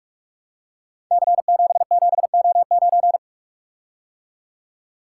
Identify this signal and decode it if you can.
Morse code, heard as a keyed tone, about 44 wpm, 700 Hz. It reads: C77O9
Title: Code Groups - 5 Letter:Numbers - 45wpm